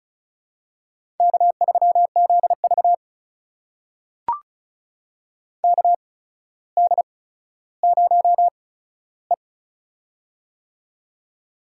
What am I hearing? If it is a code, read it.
Morse code, heard as a keyed tone, about 36 wpm, spaced as for 15 wpm, 700 Hz. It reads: K3ZV KD0E
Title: USA Call Signs - 4 Characters - 8x Character Spacing - 35wpm